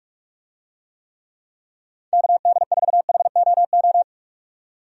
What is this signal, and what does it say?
Morse code, heard as a keyed tone, about 41 wpm, 700 Hz. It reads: KD4HOO
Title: USA Call Signs - 6 Characters - 45wpm